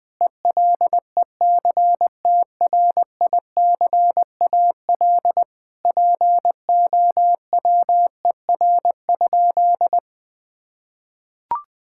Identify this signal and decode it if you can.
Morse code, heard as a keyed tone, about 21 wpm, 700 Hz. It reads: ELECTRICAL POWER?
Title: Sentences from Top 1000 Words - Review Part 6 - 20wpm